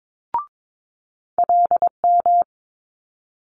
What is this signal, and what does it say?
Morse code, heard as a keyed tone, about 22 wpm, 700 Hz. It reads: LM
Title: Multi-letter Phonemes Quick Fire 22wpm